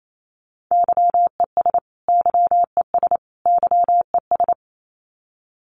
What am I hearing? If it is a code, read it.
Morse code, heard as a keyed tone, about 28 wpm, 700 Hz. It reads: YES YES YES
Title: Top 500 Words Repeated 28wpm